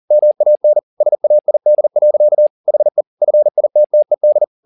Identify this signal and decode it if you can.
Morse code, heard as a keyed tone, about 40 wpm, 600 Hz. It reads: MAN SAID. HE FITTED